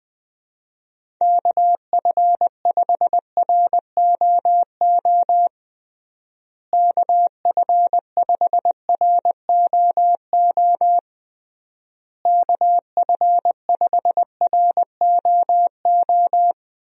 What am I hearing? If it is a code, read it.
Morse code, heard as a keyed tone, about 19 wpm, 700 Hz. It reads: KF5ROO KF5ROO KF5ROO
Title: USA Call Signs - 6 Characters - Repeated 3x - 20wpm